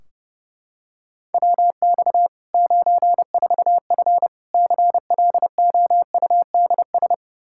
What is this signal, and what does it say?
Morse code, heard as a keyed tone, about 30 wpm, 700 Hz. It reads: WX 94F CLOUDS
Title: Practice QSO - Exchange 2 - Info - 30wpm